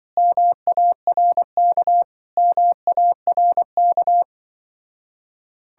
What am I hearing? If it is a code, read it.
Morse code, heard as a keyed tone, about 24 wpm, 700 Hz. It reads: MARK MARK
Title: Speed-Racing Common QSO Phrases - 20wpm